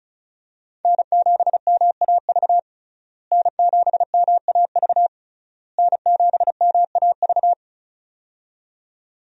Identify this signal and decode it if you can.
Morse code, heard as a keyed tone, about 36 wpm, 700 Hz. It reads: N7MAV N7MAV N7MAV
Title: USA Call Signs - 5 Characters - Repeated 3x - 35wpm